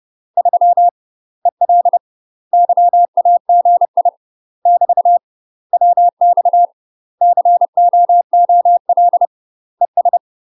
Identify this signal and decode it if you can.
Morse code, heard as a keyed tone, about 30 wpm, 700 Hz. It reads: 3 EL YAGI <BT> WX COOL ES
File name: UK MASTER QSOs TRAINING / 30WPM / UK MASTER QSOs 0035